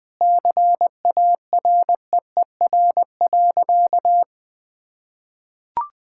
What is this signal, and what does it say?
Morse code, heard as a keyed tone, about 21 wpm, 700 Hz. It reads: CAREER.
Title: Sentences from Top 1000 Words - Review Part 6 - 20wpm